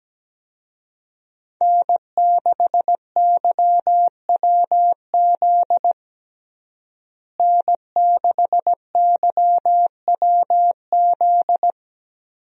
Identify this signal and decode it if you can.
Morse code, heard as a keyed tone, about 17 wpm, 700 Hz. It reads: N6YWZ N6YWZ
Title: USA Call Signs - 5 Characters - Repeated 2x - 17wpm